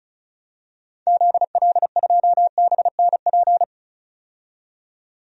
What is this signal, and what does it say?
Morse code, heard as a keyed tone, about 36 wpm, 700 Hz. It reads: ZL2BNP
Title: DXCC-100 Call Signs - 6 Characters - 35wpm